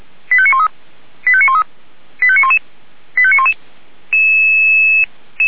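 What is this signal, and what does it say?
Five-tone selective call (ZVEI), heard as quick runs of five tones, tones of about 70 ms. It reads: 85913 85913 85911 85911
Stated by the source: Five-tone ZVEI selective call received by SDR.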